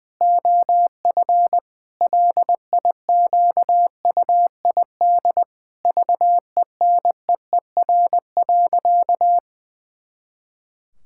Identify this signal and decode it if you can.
Morse code, heard as a keyed tone, about 19 wpm, 700 Hz. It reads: OF LIQUID VENEER.
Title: Pangram Challenge v2 - 20wpm